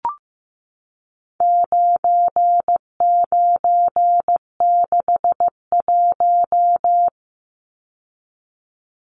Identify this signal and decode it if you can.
Morse code, heard as a keyed tone, about 15 wpm, 700 Hz. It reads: 9961
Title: Numbers - 4-Digit 15wpm